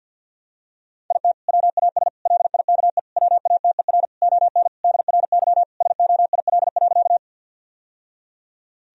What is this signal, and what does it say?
Morse code, heard as a keyed tone, about 50 wpm, 700 Hz. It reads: IT WAS LIKE WATER ON DRY SOIL.
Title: Sentences from Top 1000 Words - Review Part 4 - 50wpm